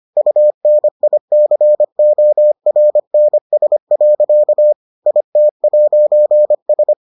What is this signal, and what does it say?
Morse code, heard as a keyed tone, about 25 wpm, 600 Hz. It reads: UNICORNS. IT'S